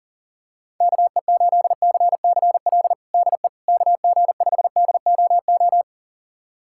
Morse code, heard as a keyed tone, about 40 wpm, 690 Hz. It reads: KE8CCL DE KG5DOO
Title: USA Callsign DE Callsign - 6 Characters - 40wpm